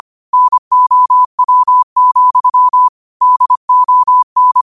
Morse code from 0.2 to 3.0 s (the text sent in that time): NOW,